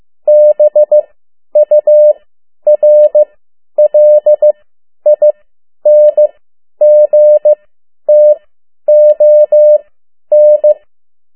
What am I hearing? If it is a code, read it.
Morse code, heard as a keyed tone, about 15 wpm, spaced as for 10 wpm, 600 Hz. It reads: BURLINGTON
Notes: Around the World in 80 Days paragraph one Morse code 10 WPM